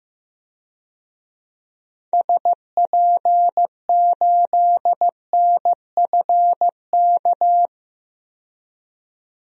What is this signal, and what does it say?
Morse code, heard as a keyed tone, about 15 wpm, 700 Hz. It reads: SP8NFK
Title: DXCC-100 Call Signs - 6 Characters - 15wpm